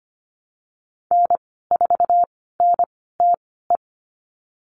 Morse code, heard as a keyed tone, about 25 wpm, spaced as for 16 wpm, 700 Hz. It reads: N4NTE
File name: Lesson 15g - Number 5 - Call Signs 25wpm